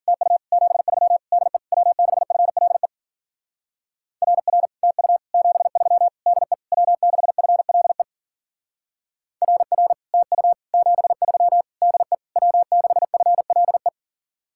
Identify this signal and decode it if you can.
Morse code, heard as a keyed tone, about 50 wpm, 700 Hz. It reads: TU 73 DE W6FLE RR TU 73 DE W6FLE RR TU 73 DE W6FLE